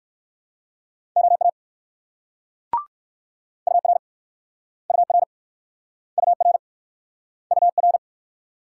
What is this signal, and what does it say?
Morse code, heard as a keyed tone, about 56 wpm, 700 Hz. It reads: OA UR UR UR UR
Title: Speed-Racing Multi-letter Phonemes - 45wpm